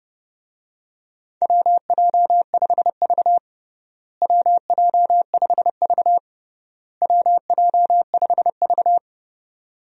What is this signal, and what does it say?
Morse code, heard as a keyed tone, about 30 wpm, 700 Hz. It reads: WJ5V WJ5V WJ5V
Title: USA Call Signs - 4 Characters - Repeated 3x - 30wpm